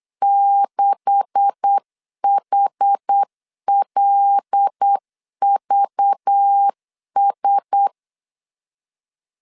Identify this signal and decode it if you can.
Morse code, heard as a keyed tone, about 8 wpm, 800 Hz. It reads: 6HLVS
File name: LEZIONI / 9 286 8wpm 3425